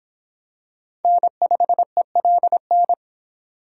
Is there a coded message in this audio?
Yes: Morse code from 1.0 to 2.9 s, heard as a keyed tone, about 25 wpm, 700 Hz. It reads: N5ELN